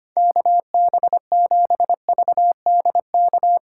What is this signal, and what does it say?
Morse code, heard as a keyed tone, about 31 wpm, 700 Hz. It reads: KB7VDK
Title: Speed Racing - USA 2x3 Call Signs 25wpm